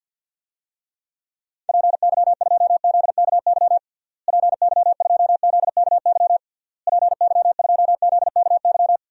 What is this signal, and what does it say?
Morse code, heard as a keyed tone, about 52 wpm, 700 Hz. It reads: PY2ZKY PY2ZKY PY2ZKY
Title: DXCC-100 Call Signs - 6 Characters - Repeated 3x - 50wpm